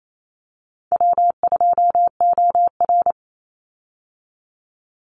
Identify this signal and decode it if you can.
Morse code, heard as a keyed tone, about 28 wpm, 700 Hz. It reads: W2OR
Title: Easy Call Signs - 28wpm